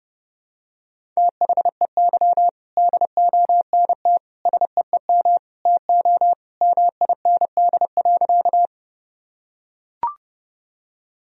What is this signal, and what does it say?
Morse code, heard as a keyed tone, about 32 wpm, 700 Hz. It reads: THEY DONT SEEM TO MIND.
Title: Sentences from Top 1000 Words - Review Part 1 - 30wpm